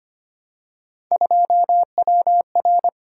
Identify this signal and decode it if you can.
Morse code, heard as a keyed tone, about 25 wpm, 700 Hz. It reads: 2WR
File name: Three Letter-Number Combinations - 25wpm